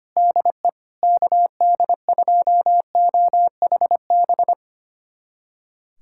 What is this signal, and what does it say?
Morse code, heard as a keyed tone, about 25 wpm, 700 Hz. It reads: DE KD2OHB
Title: USA Callsign DE Callsign - 6 Characters - 25wpm